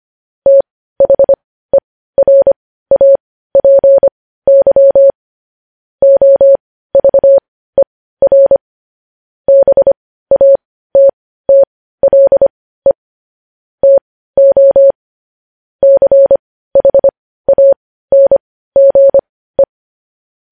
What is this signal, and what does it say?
Morse code, heard as a keyed tone, about 25 wpm, spaced as for 15 wpm, 550 Hz. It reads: THERAPY OVER BATTLE TO CHANGE
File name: podcast / News headlines-1x-25x15